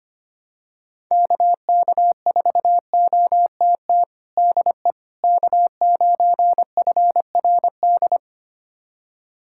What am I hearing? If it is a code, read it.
Morse code, heard as a keyed tone, about 25 wpm, 700 Hz. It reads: KK4OTT DE K9FRD
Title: USA Callsign DE Callsign 25wpm